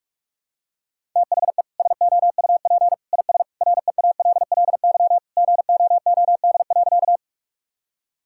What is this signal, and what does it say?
Morse code, heard as a keyed tone, about 41 wpm, 700 Hz. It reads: THE SOUP IS REALLY GOOD.
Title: Sentences from Top 1000 Words - Review Part 6 - 45wpm